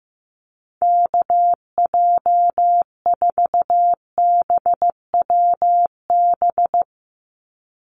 Morse code, heard as a keyed tone, about 15 wpm, 700 Hz. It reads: KJ4BWB
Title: USA Call Signs 15wpm